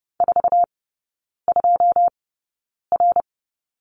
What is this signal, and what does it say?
Morse code, heard as a keyed tone, about 30 wpm, spaced as for 9 wpm, 700 Hz. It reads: 42R
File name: Lesson 18e - Letter m - Three Characters 30wpm